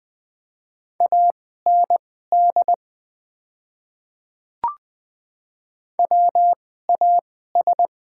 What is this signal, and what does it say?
Morse code, heard as a keyed tone, about 20 wpm, spaced as for 14 wpm, 690 Hz. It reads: AND WAS
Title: 2x Character Spacing - Top 100 Words - 20wpm